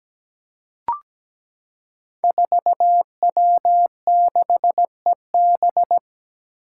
Morse code, heard as a keyed tone, about 17 wpm, 700 Hz. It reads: 4W6EB
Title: Five Challenging Letter-Number Combinations 17wpm